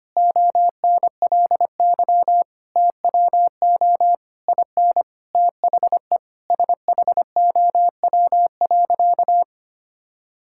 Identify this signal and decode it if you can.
Morse code, heard as a keyed tone, about 25 wpm, 700 Hz. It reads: ONLY TWO IN THE SHOW.
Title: Sentences from Top 1000 Words - Review Part 6 - 25wpm